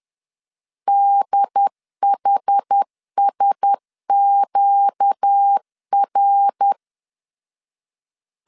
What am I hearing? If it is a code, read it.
Morse code, heard as a keyed tone, about 10 wpm, 800 Hz. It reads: DHSQR
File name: CW / DET / 1 6 1500C S B 10wpm 4137